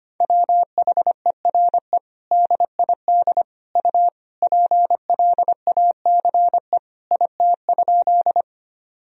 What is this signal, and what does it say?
Morse code, heard as a keyed tone, about 25 wpm, 700 Hz. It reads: WHERE DID U PLACE IT?
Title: Sentences from Top 200 Words and Abbreviations 25wpm